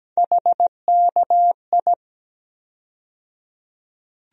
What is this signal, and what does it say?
Morse code, heard as a keyed tone, about 17 wpm, 700 Hz. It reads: HKI
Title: USA Call Sign Postfixes - 3 Letters 17wpm